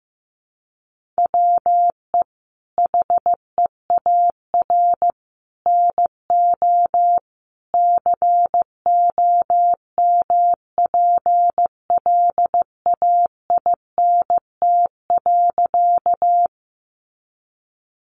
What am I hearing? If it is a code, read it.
Morse code, heard as a keyed tone, about 14 wpm, 700 Hz. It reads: WE HEAR NO COMPLAINT.
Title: Sentences from Top 1400 Words 15wpm